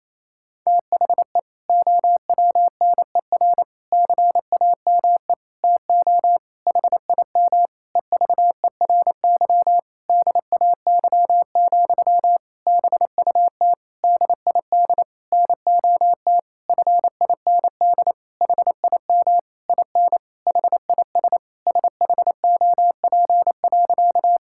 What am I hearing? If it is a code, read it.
Morse code, heard as a keyed tone, about 28 wpm, 700 Hz. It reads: THE OWNER CAME TO HIM EVERY DAY, BUT DID NOT FIND HIM IN HIS SHOP.
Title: Sentences from Top 1600 Words 28wpm